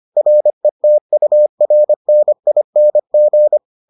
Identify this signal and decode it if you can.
Morse code, heard as a keyed tone, about 25 wpm, 600 Hz. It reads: RETURNING